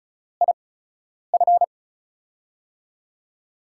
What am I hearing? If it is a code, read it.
Morse code, heard as a keyed tone, about 36 wpm, spaced as for 10 wpm, 700 Hz. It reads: IF
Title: USA Call Signs - 6 Characters - 8x Character Spacing - 35wpm